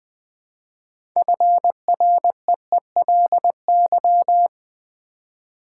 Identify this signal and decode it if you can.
Morse code, heard as a keyed tone, about 20 wpm, 700 Hz. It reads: FREELY
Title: Build A Word - 2 Letters At A Time - 6-Letter Words - 20wpm